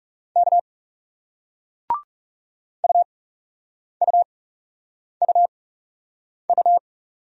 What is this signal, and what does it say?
Morse code, heard as a keyed tone, about 40 wpm, spaced as for 8 wpm, 700 Hz. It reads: K UUUU